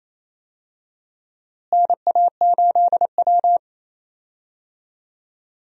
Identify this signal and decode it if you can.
Morse code, heard as a keyed tone, about 27 wpm, 700 Hz. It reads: NA8W
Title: USA Call Signs - 4 Characters - Repeated 3x - 28wpm